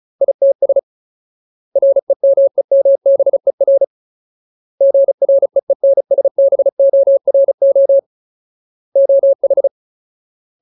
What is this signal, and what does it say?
Morse code, heard as a keyed tone, about 34 wpm, 550 Hz. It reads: ITS REMEMBER GREENSBORO OH
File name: Ultimate Word List / 35WPM Ultimate